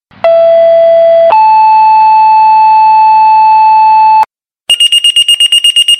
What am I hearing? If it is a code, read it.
Two-tone page, heard as a one-second tone, then a three-second tone, about 670 then 870 Hz. It is heard from 0.2 to 4.2 s.